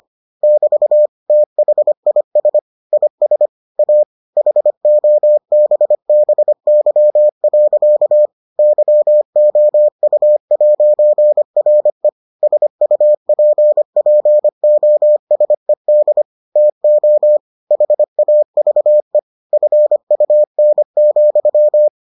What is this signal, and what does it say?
Morse code, heard as a keyed tone, about 25 wpm, 600 Hz. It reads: <BT> THIS IS A HOBBY. YOU'RE SUPPOSED TO HAVE FUN,